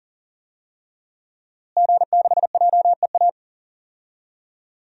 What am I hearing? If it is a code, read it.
Morse code, heard as a keyed tone, about 40 wpm, 700 Hz. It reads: G6JEA
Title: DXCC-100 Call Signs - 5 Characters - Repeated 3x - 40wpm